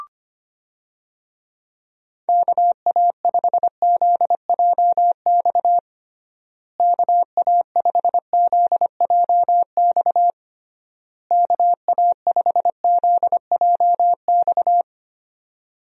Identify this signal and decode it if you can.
Morse code, heard as a keyed tone, about 25 wpm, 700 Hz. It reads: KA5ZJX KA5ZJX KA5ZJX
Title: USA Call Signs - 6 Characters - Repeated 3x - 25wpm